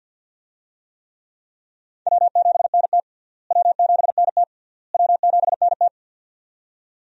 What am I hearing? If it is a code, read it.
Morse code, heard as a keyed tone, about 51 wpm, 700 Hz. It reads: W7NT W7NT W7NT
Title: USA Call Signs - 4 Characters - Repeated 3x - 50wpm